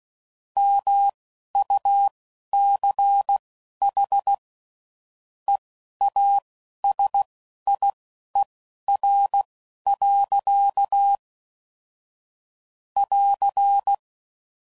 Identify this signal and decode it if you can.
Morse code, heard as a keyed tone, about 16 wpm, spaced as for 12 wpm, 800 Hz. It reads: MUCH EASIER. <AR>
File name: CW Code Groups @10WPM/Technical Subjects